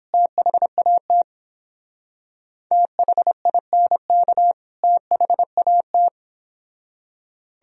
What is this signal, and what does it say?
Morse code, heard as a keyed tone, about 26 wpm, 700 Hz. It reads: THAT THINK THAT
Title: Speed-Racing - Sets of 2 Words 22wpm